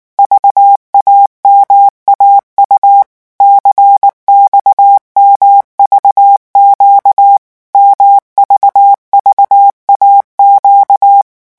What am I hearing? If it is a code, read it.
Morse code, heard as a keyed tone, about 19 wpm, 800 Hz. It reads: VAMAU CXMVQ MVVAQ
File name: morse mp3 / revision2-vitesse 18 mots